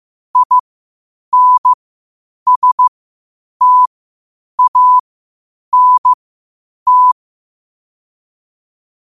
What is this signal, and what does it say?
Morse code, heard as a keyed tone, about 15 wpm, spaced as for 8 wpm, 1000 Hz. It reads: INSTANT
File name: qotdfiles / 8wpm